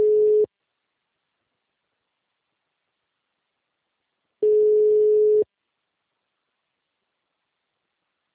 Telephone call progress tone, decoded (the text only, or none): ringback tone